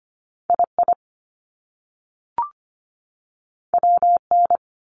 Morse code, heard as a keyed tone, about 25 wpm, spaced as for 3 wpm, 700 Hz. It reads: I IW N
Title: Top Two Letter Combinations in Words 25wpm